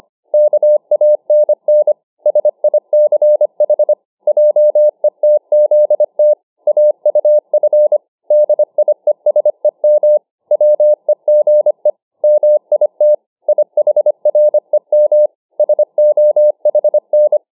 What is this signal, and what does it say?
Morse code, heard as a keyed tone, about 25 wpm, 600 Hz. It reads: KANN SICH JETZT AUF DIESEM WEGE MIT IHREM SOHN